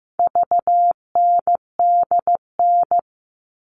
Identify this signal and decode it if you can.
Morse code, heard as a keyed tone, about 15 wpm, 700 Hz. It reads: VNDN
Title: All Cut Numbers - 4-Digit 15wpm